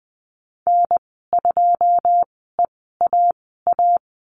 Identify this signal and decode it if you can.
Morse code, heard as a keyed tone, about 20 wpm, spaced as for 15 wpm, 700 Hz. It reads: N2EAA
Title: 161 Lesson 26g - Letter g - Call Signs 20wpm